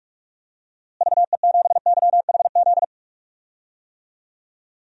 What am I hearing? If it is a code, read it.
Morse code, heard as a keyed tone, about 42 wpm, 700 Hz. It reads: VE7YHZ
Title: DXCC-100 Call Signs - 6 Characters - Repeated 3x - 45wpm